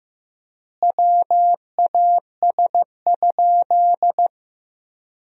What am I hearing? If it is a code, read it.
Morse code, heard as a keyed tone, about 15 wpm, 700 Hz. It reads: WAS?
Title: 3x Word-Spacing - Sentences from Top 200 Words - 15wpm